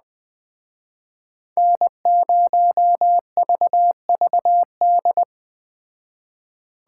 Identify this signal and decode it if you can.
Morse code, heard as a keyed tone, about 21 wpm, 700 Hz. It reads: N0VVD
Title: Speed Racing - USA 1x3 Call Signs 17wpm